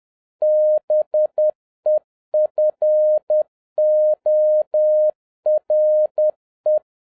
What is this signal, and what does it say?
Morse code, heard as a keyed tone, about 10 wpm, 610 Hz. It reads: BEFORE